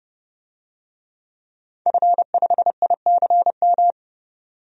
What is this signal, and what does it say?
Morse code, heard as a keyed tone, about 30 wpm, 700 Hz. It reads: F5ICM
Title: DXCC-100 Call Signs - 5 Characters - 30wpm